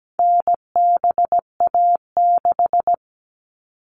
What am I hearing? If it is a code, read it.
Morse code, heard as a keyed tone, about 17 wpm, 700 Hz. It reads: NBA6
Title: All Cut Numbers - 4-Digit 17wpm